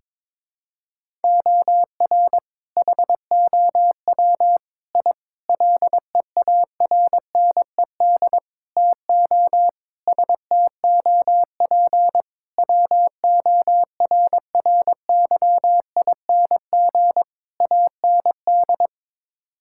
Morse code, heard as a keyed tone, about 22 wpm, 700 Hz. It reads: OR HOW I LEARNED TO STOP WORRYING AND
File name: practice / 22wpm / Sets of 8 Words - 22wpm